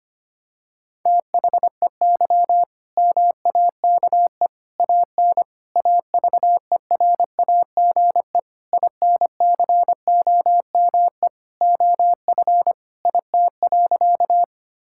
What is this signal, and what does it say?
Morse code, heard as a keyed tone, about 25 wpm, 700 Hz. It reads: THEY MAKE AN AVERAGE INCOME OF IT.
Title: Sentences from Top 1000 Words - Review Part 6 - 25wpm